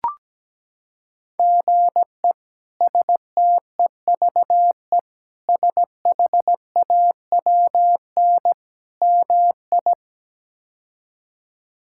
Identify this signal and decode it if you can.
Morse code, heard as a keyed tone, about 17 wpm, 700 Hz. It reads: GE STEVE SHAWN MI
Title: K1USN SST QSO Practice - 17wpm